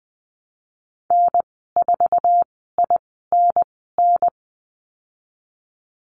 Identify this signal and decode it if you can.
Morse code, heard as a keyed tone, about 20 wpm, spaced as for 15 wpm, 700 Hz. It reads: N4INN
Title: 082 Lesson 14g - Number 2 - Call Signs 20wpm